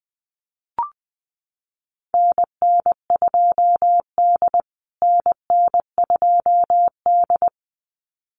Morse code, heard as a keyed tone, about 20 wpm, 700 Hz. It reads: NN2D NN2D
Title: USA Call Signs Repeated 2x 20wpm